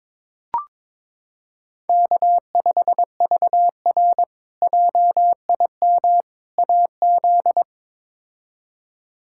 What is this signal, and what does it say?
Morse code, heard as a keyed tone, about 22 wpm, 700 Hz. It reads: K5VR JIM AZ
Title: K1USN SST QSO Practice - 22wpm